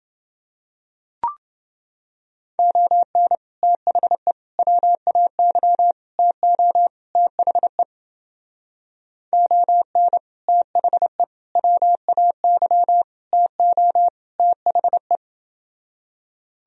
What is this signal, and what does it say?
Morse code, heard as a keyed tone, about 29 wpm, 700 Hz. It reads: ON THE WAY TO THE ON THE WAY TO THE